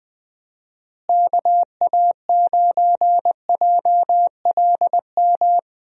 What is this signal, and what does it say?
Morse code, heard as a keyed tone, about 21 wpm, 700 Hz. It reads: KA9JLM
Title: USA Call Signs - 6 Characters - Repeated 2x - 20wpm